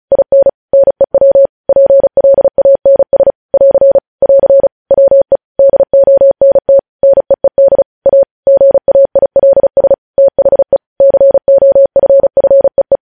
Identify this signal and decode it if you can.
Morse code, heard as a keyed tone, about 36 wpm, 550 Hz. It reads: IN NEW PLANS <AR> <AR> WE DONT NEED A GAILS THE COFFEE
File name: podcast / News headlines-1x-35x35